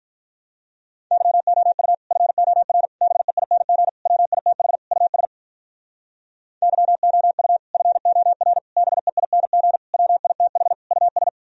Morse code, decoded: YOU FOR BEING WITH US YOU FOR BEING WITH US

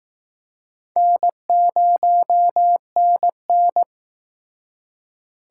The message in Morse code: N0NN

